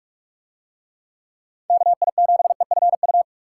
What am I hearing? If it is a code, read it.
Morse code, heard as a keyed tone, about 46 wpm, 700 Hz. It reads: KI7EFU